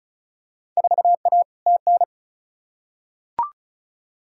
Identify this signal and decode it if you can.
Morse code, heard as a keyed tone, about 34 wpm, 700 Hz. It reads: 4A TN
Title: Field Day Exchanges - 35wpm